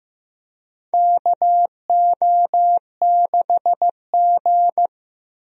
Morse code, heard as a keyed tone, about 15 wpm, 700 Hz. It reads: KO6G
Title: USA Call Signs - 4 Characters - Repeated 3x - 15wpm